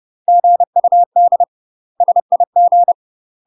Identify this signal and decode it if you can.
Morse code, heard as a keyed tone, about 30 wpm, 700 Hz. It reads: GUD SIG
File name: UK MASTER QSOs TRAINING / 30WPM / UK MASTER QSOs 0000